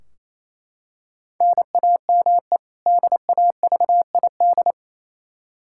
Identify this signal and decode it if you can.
Morse code, heard as a keyed tone, about 28 wpm, 700 Hz. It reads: NAME DAVID
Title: K1USN SST QSO Practice 28wpm